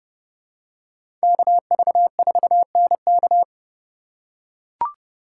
Morse code, heard as a keyed tone, about 30 wpm, 690 Hz. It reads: KV4NK
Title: USA Call Signs - 3x Character Spacing - 30wpm